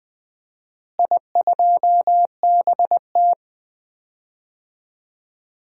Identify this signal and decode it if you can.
Morse code, heard as a keyed tone, about 20 wpm, 700 Hz. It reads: I2BT
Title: DXCC-100 Call Signs - 4 Characters - Repeated 2x - 20wpm